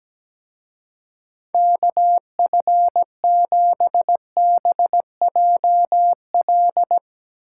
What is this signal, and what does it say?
Morse code, heard as a keyed tone, about 18 wpm, 700 Hz. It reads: KF7BJL
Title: Speed Racing - USA 2x3 Call Signs 17wpm